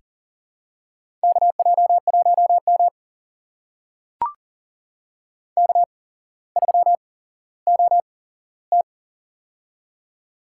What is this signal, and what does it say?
Morse code, heard as a keyed tone, about 40 wpm, spaced as for 17 wpm, 700 Hz. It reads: KJ1M K3OT
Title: USA Call Signs - 4 Characters - 8x Character Spacing - 40wpm